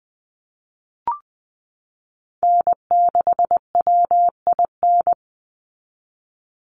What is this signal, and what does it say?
Morse code, heard as a keyed tone, about 20 wpm, 700 Hz. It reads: N6WIN
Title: Call Signs Repeated 2x - 20wpm